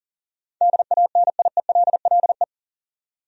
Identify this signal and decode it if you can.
Morse code, heard as a keyed tone, about 40 wpm, 700 Hz. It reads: DANIELLE
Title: USA Common Names - 40wpm